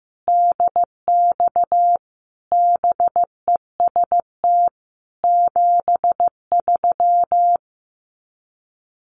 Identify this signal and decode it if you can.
Morse code, heard as a keyed tone, about 15 wpm, 700 Hz. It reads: DX BEST 73
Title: Common QSO Phrases 15wpm